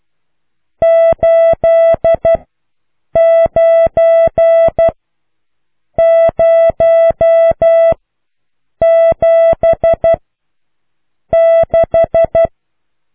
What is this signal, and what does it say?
Morse code, heard as a keyed tone, about 12 wpm, spaced as for 6 wpm, 660 Hz. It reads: 89076